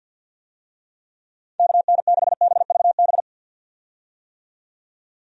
Morse code, heard as a keyed tone, about 52 wpm, 700 Hz. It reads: KN6BVB
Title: USA Call Signs - 6 Characters - 50wpm